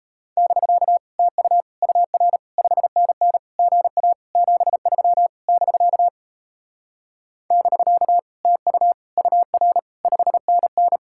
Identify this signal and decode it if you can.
Morse code, heard as a keyed tone, about 36 wpm, 700 Hz. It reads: <BK> TU UR 5NN GA 73 <BK> <BK> TU UR 5NN